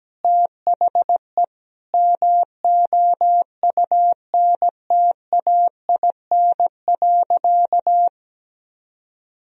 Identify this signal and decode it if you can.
Morse code, heard as a keyed tone, about 17 wpm, 700 Hz. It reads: THE MOUNTAIN.
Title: Sentences from Top 1000 Words - Review Part 3 - 17wpm